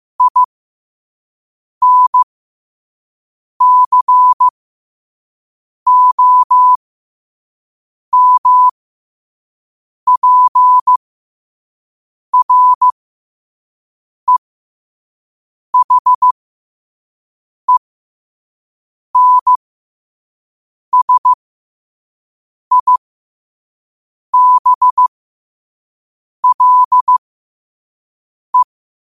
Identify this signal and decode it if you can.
Morse code, heard as a keyed tone, about 15 wpm, spaced as for 5 wpm, 1000 Hz. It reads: INCOMPREHENSIBLE
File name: 5wpm